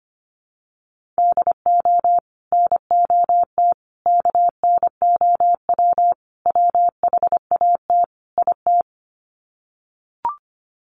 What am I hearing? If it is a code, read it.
Morse code, heard as a keyed tone, about 25 wpm, 700 Hz. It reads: DO NOT KNOW WHAT IT
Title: Sets of 5 Words 25wpm